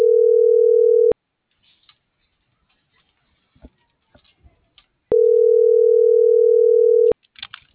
Ringback tone (North America), 440 + 480 Hz.